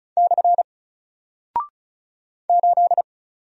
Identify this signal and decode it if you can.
Morse code, heard as a keyed tone, about 36 wpm, 700 Hz. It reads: / 8
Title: Single Character (Letter:Number:Punctuation:Prosign) Alt 35wpm